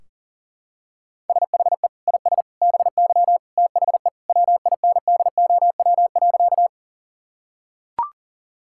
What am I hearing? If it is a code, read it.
Morse code, heard as a keyed tone, about 40 wpm, 700 Hz. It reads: SHE IS BY THE WINDOW.
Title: Sentences from Top 1600 Words 40wpm